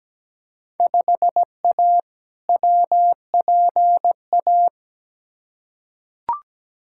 Morse code, heard as a keyed tone, about 17 wpm, 700 Hz. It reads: 5A WPA
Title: Field Day Exchanges Encore - 17wpm